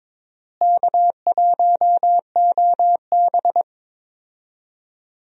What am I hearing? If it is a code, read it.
Morse code, heard as a keyed tone, about 22 wpm, 700 Hz. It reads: K1OB
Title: USA Call Signs - 4 Characters - Repeated 2x - 22wpm